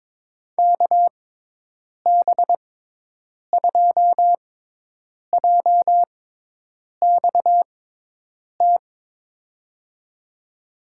Morse code, heard as a keyed tone, about 22 wpm, spaced as for 8 wpm, 700 Hz. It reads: KB2JXT
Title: USA Call Signs - 6x Character Spacing - 22wpm